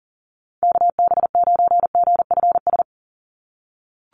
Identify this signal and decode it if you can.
Morse code, heard as a keyed tone, about 40 wpm, 700 Hz. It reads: KB9GFS
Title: USA Call Signs 40wpm